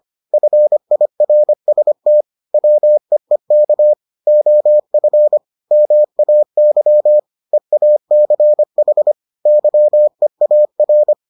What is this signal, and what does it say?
Morse code, heard as a keyed tone, about 25 wpm, 600 Hz. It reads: FIRST WEEK OF MAY EACH YEAR